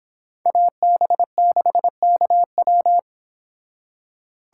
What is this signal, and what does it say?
Morse code, heard as a keyed tone, about 25 wpm, 700 Hz. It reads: AB6KW